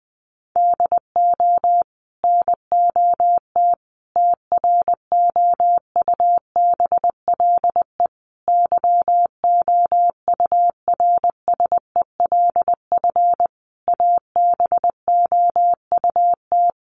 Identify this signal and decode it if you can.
Morse code, heard as a keyed tone, about 20 wpm, 700 Hz. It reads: DO NOT TROUBLE YOURSELF ABOUT
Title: Sentences from Top 1200 Words 20wpm